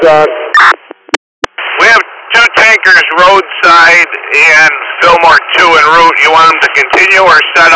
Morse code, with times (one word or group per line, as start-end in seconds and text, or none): none